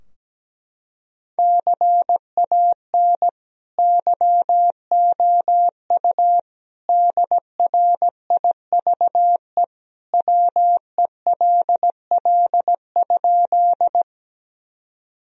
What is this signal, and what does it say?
Morse code, heard as a keyed tone, about 17 wpm, 700 Hz. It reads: CAN YOU DRIVE WELL?
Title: Sentences from Top 1000 Words - Review Part 2 - 17wpm